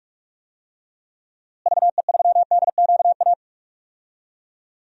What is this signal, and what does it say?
Morse code, heard as a keyed tone, about 42 wpm, 700 Hz. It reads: VE3DQA
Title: DXCC-100 Call Signs - 6 Characters - Repeated 3x - 45wpm